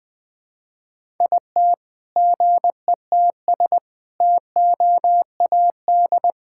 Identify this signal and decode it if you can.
Morse code, heard as a keyed tone, about 20 wpm, 700 Hz. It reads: IT GETS TOAD
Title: All-Age Appropriate Jokes - 20wpm